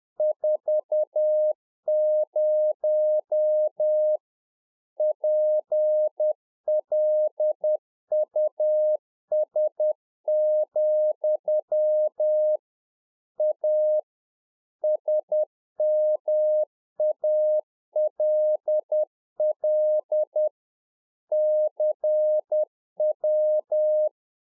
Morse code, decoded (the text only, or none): 40 PLUS, A SMALL CW